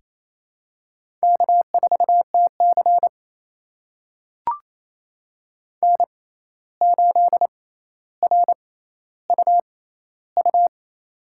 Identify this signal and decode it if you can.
Morse code, heard as a keyed tone, about 27 wpm, spaced as for 13 wpm, 700 Hz. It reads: K4TC N8RUU
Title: USA Call Signs - 6x Character Spacing - 28wpm